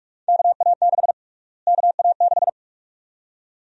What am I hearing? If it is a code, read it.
Morse code, heard as a keyed tone, about 46 wpm, 700 Hz. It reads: KA6 KA6
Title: USA Call Sign Prefixes - Repeated 2x - 45wpm